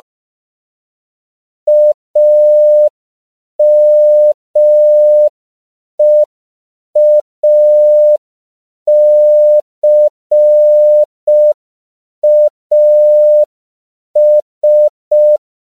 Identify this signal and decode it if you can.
Morse code, heard as a keyed tone, about 5 wpm, 600 Hz. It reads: AMEACAS
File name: CW / 5PPM